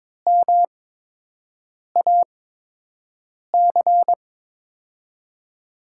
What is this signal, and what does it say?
Morse code, heard as a keyed tone, about 23 wpm, spaced as for 6 wpm, 700 Hz. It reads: MAC
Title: USA Call Signs - 8x Character Spacing - 22wpm